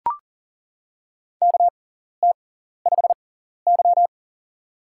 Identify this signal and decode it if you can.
Morse code, heard as a keyed tone, about 39 wpm, spaced as for 14 wpm, 700 Hz. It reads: KT5Y
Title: USA Call Signs - 6x Character Spacing - 40wpm